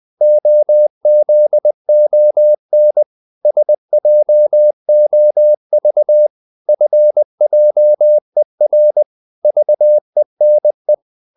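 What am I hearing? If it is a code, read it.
Morse code, heard as a keyed tone, about 20 wpm, 600 Hz. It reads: OZON SJOV FJER VENE
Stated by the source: Morse code practice of short Danish words at 20 wpm, from CW med Gnister.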